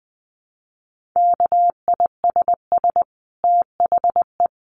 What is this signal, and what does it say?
Morse code, heard as a keyed tone, about 20 wpm, 700 Hz. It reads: KISS THE
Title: Dit Challenge 20wpm